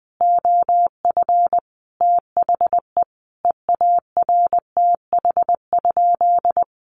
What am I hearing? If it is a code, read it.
Morse code, heard as a keyed tone, about 20 wpm, 700 Hz. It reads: OF THE EARTH?
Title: Sentences from Top 1200 Words 20wpm